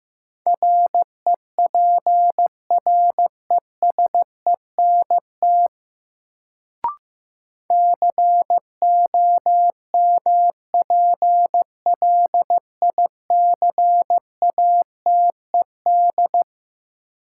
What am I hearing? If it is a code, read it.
Morse code, heard as a keyed tone, about 15 wpm, 700 Hz. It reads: REPRESENT COMPLICATED
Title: Sentences from Top 1200 Words - New Word Drill Alt - 15wpm